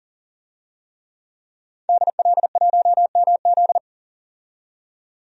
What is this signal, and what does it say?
Morse code, heard as a keyed tone, about 40 wpm, 700 Hz. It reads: DL1MZ
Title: DXCC-100 Call Signs - 5 Characters - 40wpm